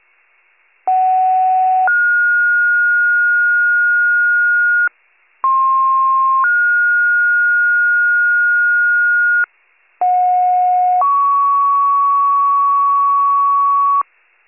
Two-tone page, heard as a one-second tone, then a three-second tone, about 750 then 1430 Hz, about 1060 then 1430 Hz, about 730 then 1120 Hz, with noise. It is heard from 0.9 to 14.0 s.